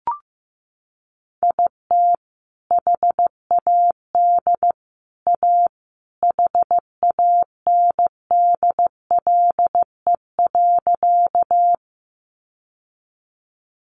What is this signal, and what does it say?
Morse code, heard as a keyed tone, about 15 wpm, 700 Hz. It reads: IT HAD A HANDLE.
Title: Sentences from Top 1100 Words 15wpm